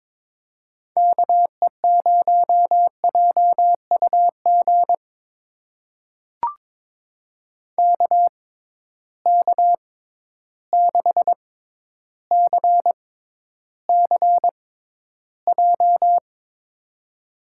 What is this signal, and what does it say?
Morse code, heard as a keyed tone, about 22 wpm, spaced as for 11 wpm, 700 Hz. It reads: KE0JUG KK6CCJ
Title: USA Call Signs - 6 Characters - 6x Character Spacing - 22wpm